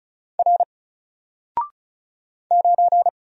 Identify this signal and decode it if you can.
Morse code, heard as a keyed tone, about 34 wpm, 700 Hz. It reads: R 9
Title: Single Letter-Number 35wpm